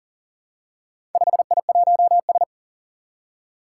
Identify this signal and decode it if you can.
Morse code, heard as a keyed tone, about 40 wpm, 700 Hz. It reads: 5I1S
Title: Four Challenging Letter-Number Combinations 40wpm